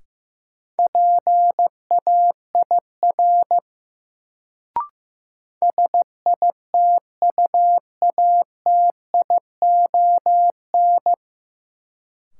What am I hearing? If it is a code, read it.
Morse code, heard as a keyed tone, about 15 wpm, 700 Hz. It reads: PAIR SITUATION
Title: Sentences from Top 1200 Words - New Word Drill - 15wpm